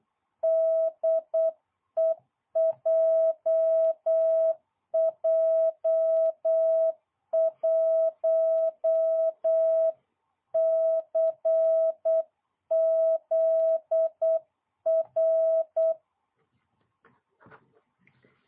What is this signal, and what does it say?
Morse code, heard as a keyed tone, about 8 wpm, 660 Hz. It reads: DE JJ1CZR